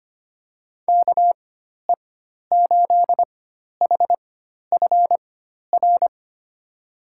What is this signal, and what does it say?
Morse code, heard as a keyed tone, about 25 wpm, spaced as for 12 wpm, 700 Hz. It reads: KE8HFR
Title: USA Call Signs - 4x Character Spacing - 25wpm